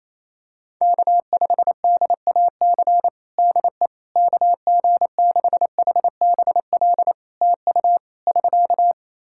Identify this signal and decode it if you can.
Morse code, heard as a keyed tone, about 28 wpm, 700 Hz. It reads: K5DAC DE KG6HBL TU <SK>
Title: K1USN SST QSO Practice 28wpm